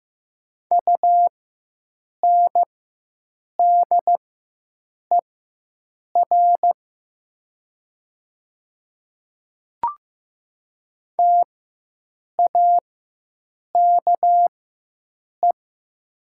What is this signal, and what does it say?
Morse code, heard as a keyed tone, about 15 wpm, spaced as for 7 wpm, 700 Hz. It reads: UNDER TAKE
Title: 4x Character Spacing - Top 200 Words - 15wpm